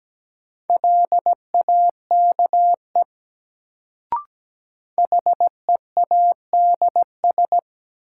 Morse code, heard as a keyed tone, about 17 wpm, 700 Hz. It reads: LAKE HEADS
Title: Sentences from Top 1000 Words - Review Part 3 - New Word Drill - 17wpm